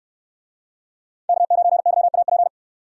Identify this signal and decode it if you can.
Morse code, heard as a keyed tone, about 68 wpm, 700 Hz. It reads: K9YAP